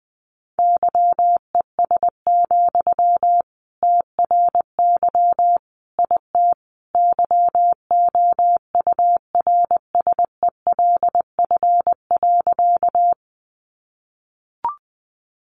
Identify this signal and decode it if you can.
Morse code, heard as a keyed tone, about 20 wpm, 700 Hz. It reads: YES, TRY IT YOURSELF.
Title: Sentences from Top 600 Words 20wpm